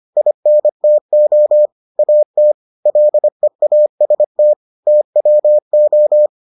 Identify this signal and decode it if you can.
Morse code, heard as a keyed tone, about 25 wpm, 600 Hz. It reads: INTO AT LEAST TWO